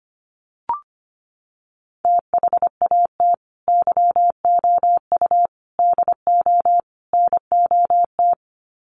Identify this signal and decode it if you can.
Morse code, heard as a keyed tone, about 25 wpm, 700 Hz. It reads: THAT YOU DO NOT
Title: Sets of 4 Words 25wpm